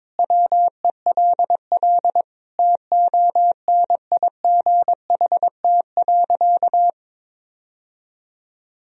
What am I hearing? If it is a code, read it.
Morse code, heard as a keyed tone, about 22 wpm, 700 Hz. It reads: WELL TONIGHT.
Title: Sentences from Top 1000 Words - Review Part 4 - 22wpm